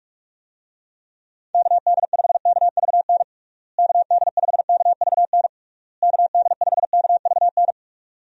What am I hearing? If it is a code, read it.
Morse code, heard as a keyed tone, about 43 wpm, 700 Hz. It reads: KD5KVN KD5KVN KD5KVN
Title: USA Call Signs - 6 Characters - Repeated 3x - 45wpm